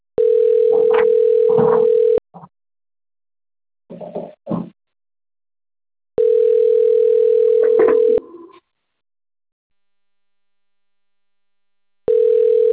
Ringback tone (North America), 440 + 480 Hz.